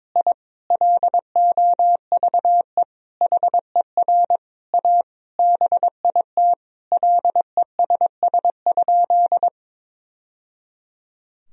Morse code, heard as a keyed tone, about 22 wpm, 700 Hz. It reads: I LOVE HER A BIT LESS?
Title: Sentences from Top 1000 Words - Review Part 2 - 22wpm